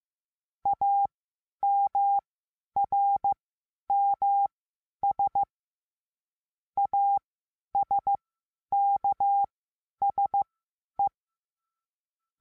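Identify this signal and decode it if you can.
Morse code, heard as a keyed tone, about 15 wpm, spaced as for 10 wpm, 800 Hz. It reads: AMRMS ASKSE
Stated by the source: Morse lesson adding the letter A, sending random five-letter practice groups.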